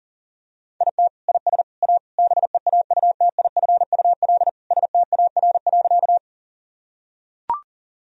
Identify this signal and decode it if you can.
Morse code, heard as a keyed tone, about 40 wpm, 700 Hz. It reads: IT IS A BEAUTIFUL STAR.
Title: Sentences from Top 1000 Words - Review Part 2 - 40wpm